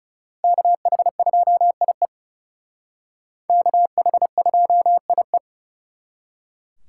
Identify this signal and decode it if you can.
Morse code, heard as a keyed tone, about 36 wpm, 700 Hz. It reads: KH2IE KH2IE
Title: Speed Racing - USA 2x2 Call Signs 30wpm